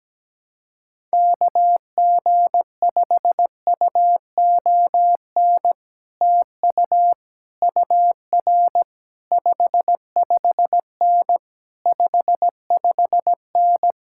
Morse code, heard as a keyed tone, about 17 wpm, 700 Hz. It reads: KG5UON TU UR 55N 55N
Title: Speed-Racing Common QSO Phrases - 17wpm